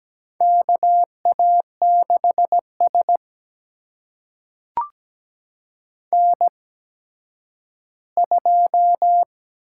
Morse code, heard as a keyed tone, about 17 wpm, spaced as for 10 wpm, 700 Hz. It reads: KA6S N2
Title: USA Call Signs - 4 Characters - 8x Character Spacing - 17wpm